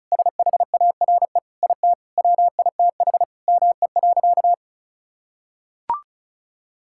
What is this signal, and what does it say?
Morse code, heard as a keyed tone, about 36 wpm, 700 Hz. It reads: SHARE IT WITH ME.
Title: Sentences from Top 1000 Words - Review Part 6 - 35wpm